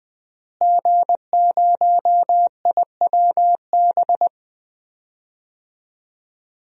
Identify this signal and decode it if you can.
Morse code, heard as a keyed tone, about 20 wpm, 700 Hz. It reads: G0IWB
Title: DXCC-100 Call Signs - 5 Characters - Repeated 2x - 20wpm